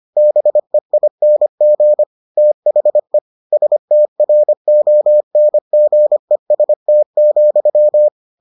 Morse code, heard as a keyed tone, about 25 wpm, 600 Hz. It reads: BEING THE STRONGEST,